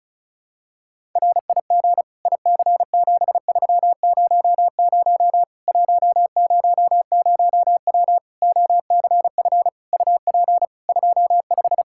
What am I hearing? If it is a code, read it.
Morse code, heard as a keyed tone, about 36 wpm, 700 Hz. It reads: RIG IC7300 100W OCF UP 25
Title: Practice QSO - Exchange 2 - Info - 35wpm